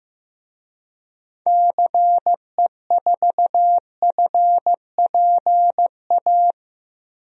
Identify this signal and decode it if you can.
Morse code, heard as a keyed tone, about 16 wpm, 700 Hz. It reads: CE4FPA
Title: DXCC-100 Call Signs - 6 Characters - Repeated 2x - 15wpm